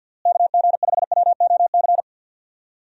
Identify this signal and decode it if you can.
Morse code, heard as a keyed tone, about 50 wpm, 700 Hz. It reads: KG5WOC
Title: USA Call Signs - 6 Characters - Repeated 3x - 50wpm